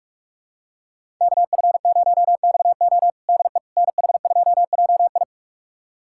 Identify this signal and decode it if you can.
Morse code, heard as a keyed tone, about 46 wpm, 700 Hz. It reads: KF0XO DE NH2JI
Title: USA Callsign DE Callsign - 5 Characters - 45wpm